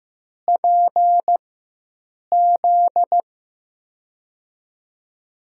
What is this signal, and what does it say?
Morse code, heard as a keyed tone, about 15 wpm, 700 Hz. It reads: P Z
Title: USA Call Signs - 6 Characters - 4x Character Spacing - 15wpm